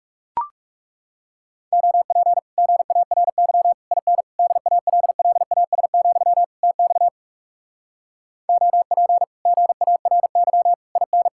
Morse code, heard as a keyed tone, about 40 wpm, 700 Hz. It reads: OP GARY IN DALLAS, TX OP GARY IN